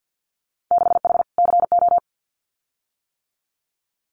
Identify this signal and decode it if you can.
Morse code, heard as a keyed tone, about 50 wpm, 700 Hz. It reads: 6H CO
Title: Winter Field Day Exchanges Encore 50wpm